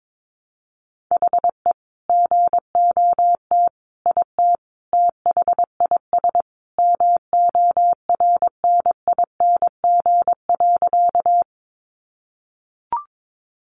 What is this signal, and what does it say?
Morse code, heard as a keyed tone, about 22 wpm, 700 Hz. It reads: HE GOT IT THIS MORNING.
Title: Sentences from Top 400 Words 22wpm